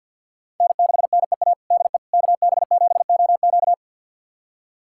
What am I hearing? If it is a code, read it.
Morse code, heard as a keyed tone, about 51 wpm, 690 Hz. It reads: N6NEA DE KB7OQ
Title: USA Callsign DE Callsign - 5 Characters - 50wpm